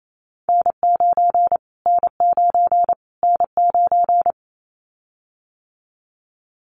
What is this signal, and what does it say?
Morse code, heard as a keyed tone, about 28 wpm, 700 Hz. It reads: N9 N9 N9
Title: Callsign Prefixes for Top 25 DXCC - Repeated 3x - 28wpm